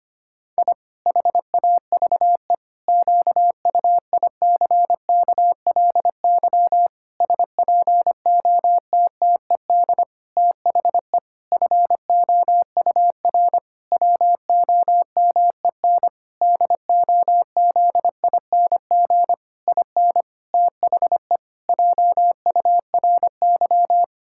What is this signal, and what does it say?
Morse code, heard as a keyed tone, about 25 wpm, 700 Hz. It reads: I HAVE QUICKLY SPOTTED THE FOUR WOMEN DOZING IN THE JURY
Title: Pangram Challenge v2 - 25wpm